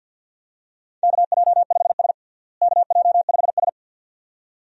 Morse code, heard as a keyed tone, about 48 wpm, 700 Hz. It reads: KJ5S KJ5S
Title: USA Call Signs - 4 Characters - Repeated 2x - 50wpm